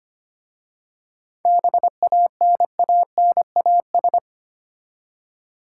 Morse code, heard as a keyed tone, about 26 wpm, 700 Hz. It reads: BANANAS
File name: practice / 25wpm / Easy Words with Hints - 25wpm